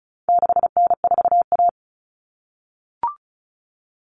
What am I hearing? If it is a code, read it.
Morse code, heard as a keyed tone, about 34 wpm, 700 Hz. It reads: 6N4A
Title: Common Cut Numbers - 4-Digit 35wpm